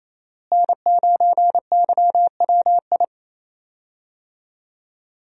Morse code, heard as a keyed tone, about 28 wpm, 700 Hz. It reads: N9YWI
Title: USA Call Signs - 5 Characters - Repeated 3x - 28wpm